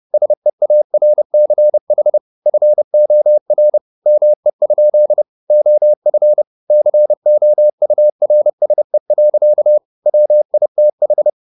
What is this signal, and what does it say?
Morse code, heard as a keyed tone, about 30 wpm, 600 Hz. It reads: SEARCH FOR ME? OF COURSE. WITH